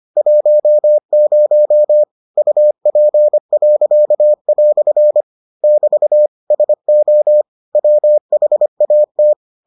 Morse code, heard as a keyed tone, about 25 wpm, 600 Hz. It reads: 10 UP." <BT> SO WHAT